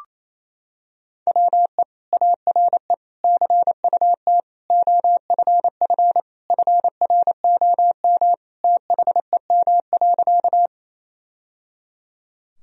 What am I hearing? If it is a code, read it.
Morse code, heard as a keyed tone, about 28 wpm, 700 Hz. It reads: WE ARE CUT OFF FROM THEM.
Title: Sentences from Top 1000 Words - Review Part 1 - 28wpm